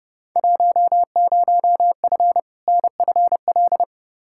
Morse code, heard as a keyed tone, about 31 wpm, 700 Hz. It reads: 10F NFL
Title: Field Day Exchanges Encore - 30wpm